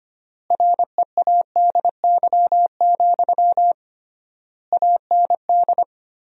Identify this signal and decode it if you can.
Morse code, heard as a keyed tone, about 25 wpm, 700 Hz. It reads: READY, AND
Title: 3x Word-Spacing - Sentences from Top 500 Words - 25wpm